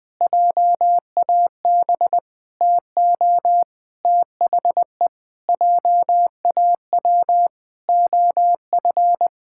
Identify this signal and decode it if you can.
Morse code, heard as a keyed tone, about 20 wpm, 700 Hz. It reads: JAB TO THE JAW OF
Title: Pangram Challenge v2 - 20wpm